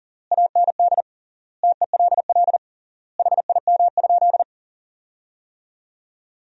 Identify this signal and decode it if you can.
Morse code, heard as a keyed tone, about 40 wpm, 700 Hz. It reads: AND TELL HIM?
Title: 3x Word-Spacing - Sentences from Top 200 Words - 40wpm